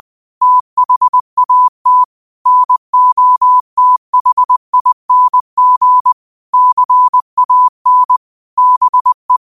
Morse code, heard as a keyed tone, about 20 wpm, 1000 Hz. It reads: THAT NOTHING CAN BE
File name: qotdfiles / 20wpm